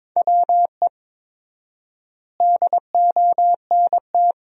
Morse code, heard as a keyed tone, about 22 wpm, 700 Hz. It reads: WE DONT
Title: 4x Word-Spacing - Sentences from Top 300 Words - 22wpm